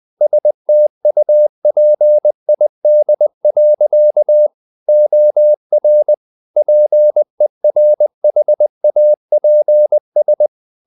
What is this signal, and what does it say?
Morse code, heard as a keyed tone, about 20 wpm, 600 Hz. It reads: STUPID. OR PERHAPS
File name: the old man and the sea / the old man and the sea part 1 20wpm